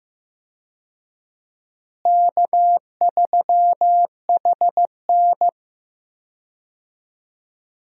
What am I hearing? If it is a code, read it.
Morse code, heard as a keyed tone, about 15 wpm, 700 Hz. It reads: K3HN
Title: USA Call Signs - 4 Characters - Repeated 3x - 15wpm